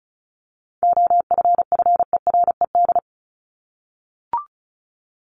Morse code, heard as a keyed tone, about 34 wpm, 700 Hz. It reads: OFFERED
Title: ED Suffix - Top 100 Words with Base Words 35wpm